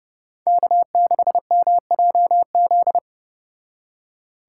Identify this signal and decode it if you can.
Morse code, heard as a keyed tone, about 29 wpm, 700 Hz. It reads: K6MJZ